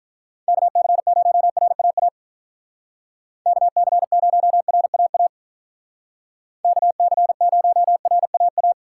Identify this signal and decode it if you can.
Morse code, heard as a keyed tone, about 43 wpm, 700 Hz. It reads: KC0RAA KC0RAA KC0RAA
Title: Speed Racing - USA 2x3 Call Signs 35wpm